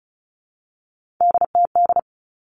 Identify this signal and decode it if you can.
Morse code, heard as a keyed tone, about 34 wpm, 700 Hz. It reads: DTD
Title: All Cut Numbers - 3-Digit 35wpm